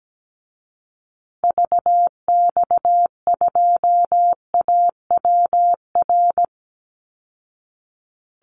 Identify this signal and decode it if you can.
Morse code, heard as a keyed tone, about 17 wpm, 700 Hz. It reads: VX2AWR
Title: Call Signs Repeated 2x - 17wpm